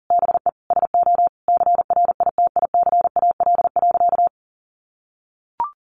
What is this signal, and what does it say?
Morse code, heard as a keyed tone, about 37 wpm, 700 Hz. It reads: BE SO CRITICAL.
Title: Sentences from Top 1100 Words 40wpm